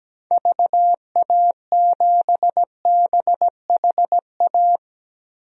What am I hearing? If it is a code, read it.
Morse code, heard as a keyed tone, about 17 wpm, 700 Hz. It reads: VA7BHA
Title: DXCC-100 Call Signs - 6 Characters - 17wpm